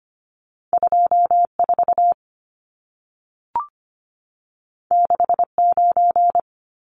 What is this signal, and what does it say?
Morse code, heard as a keyed tone, about 25 wpm, 700 Hz. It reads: 24 69
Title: Numbers - 2-Digit 25wpm